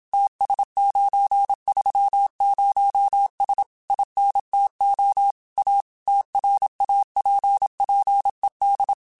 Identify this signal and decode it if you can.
Morse code, heard as a keyed tone, about 27 wpm, 800 Hz. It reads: TS930S INTO A TRAPPED